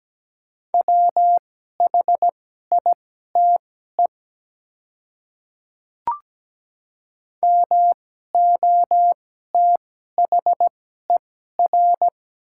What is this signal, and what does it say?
Morse code, heard as a keyed tone, about 17 wpm, spaced as for 12 wpm, 700 Hz. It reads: WHITE MOTHER
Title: 2x Character Spacing - Top 300 Words - 17wpm